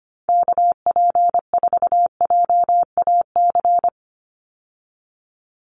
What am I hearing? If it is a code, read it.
Morse code, heard as a keyed tone, about 25 wpm, 700 Hz. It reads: KP4JAC
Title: USA Call Signs 25wpm